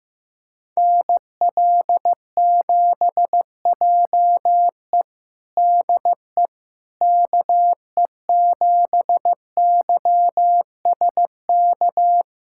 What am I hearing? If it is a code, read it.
Morse code, heard as a keyed tone, about 15 wpm, 700 Hz. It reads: NL7JE DE KE7YSK
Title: USA Callsign DE Callsign 15wpm